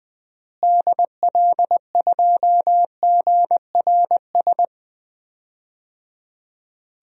Morse code, heard as a keyed tone, about 21 wpm, 700 Hz. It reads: DL2GRS
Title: DXCC-100 Call Signs - 6 Characters - Repeated 3x - 20wpm